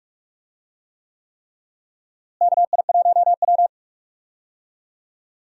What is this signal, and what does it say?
Morse code, heard as a keyed tone, about 44 wpm, 700 Hz. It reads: KI1W
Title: USA Call Signs - 4 Characters - Repeated 3x - 45wpm